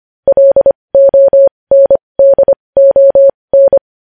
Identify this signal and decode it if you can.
Morse code, heard as a keyed tone, about 25 wpm, spaced as for 20 wpm, 550 Hz. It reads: LONDON
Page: news-headlines-in-morse-code-at-20-wpm